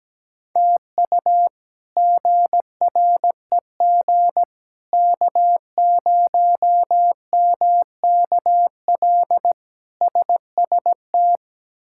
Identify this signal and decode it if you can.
Morse code, heard as a keyed tone, about 17 wpm, 700 Hz. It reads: TU GREG K0MKL SST
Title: K1USN SST QSO Practice - 17wpm